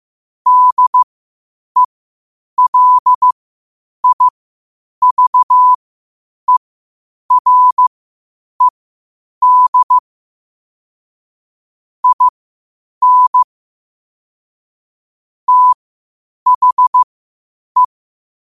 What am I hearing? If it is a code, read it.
Morse code, heard as a keyed tone, about 15 wpm, spaced as for 8 wpm, 1000 Hz. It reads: DELIVERED IN THE
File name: qotdfiles / 8wpm